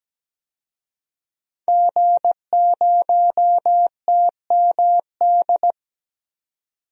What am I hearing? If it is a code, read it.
Morse code, heard as a keyed tone, about 17 wpm, 700 Hz. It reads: G0TMD
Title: DXCC-100 Call Signs - 5 Characters - 17wpm